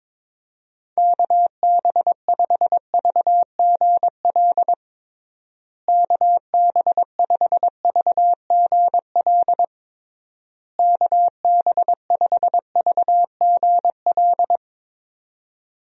Morse code, heard as a keyed tone, about 21 wpm, 700 Hz. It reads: KB5VGL KB5VGL KB5VGL
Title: USA Call Signs - 6 Characters - Repeated 3x - 22wpm